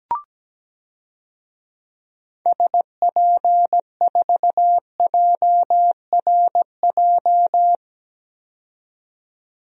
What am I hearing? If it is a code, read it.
Morse code, heard as a keyed tone, about 17 wpm, 700 Hz. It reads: SP4JRJ
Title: DXCC-100 Call Signs - 6 Characters - 17wpm